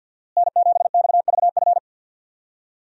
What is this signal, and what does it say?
Morse code, heard as a keyed tone, about 50 wpm, 700 Hz. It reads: N7XVF
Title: USA Call Signs - 5 Characters - Repeated 2x - 50wpm